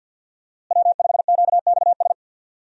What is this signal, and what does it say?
Morse code, heard as a keyed tone, about 46 wpm, 700 Hz. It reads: W5QXS
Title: USA Call Signs - 5 Characters - Repeated 3x - 50wpm